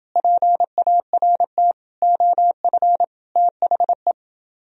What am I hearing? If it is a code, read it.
Morse code, heard as a keyed tone, about 30 wpm, 700 Hz. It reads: PART OF THE